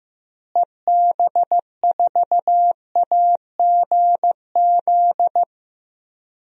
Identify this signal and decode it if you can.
Morse code, heard as a keyed tone, about 15 wpm, 700 Hz. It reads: EB4AGZ
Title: DXCC-100 Call Signs - 6 Characters - Repeated 3x - 15wpm